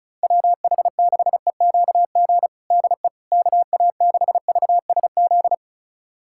Morse code, heard as a keyed tone, about 34 wpm, 690 Hz. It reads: WH6EQG DE KA6VSZ
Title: USA Callsign DE Callsign - 6 Characters - 35wpm